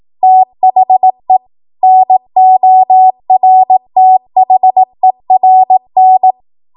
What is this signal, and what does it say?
Morse code, heard as a keyed tone, about 18 wpm, 750 Hz. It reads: THE NORTHERN
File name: Morse / 18WPM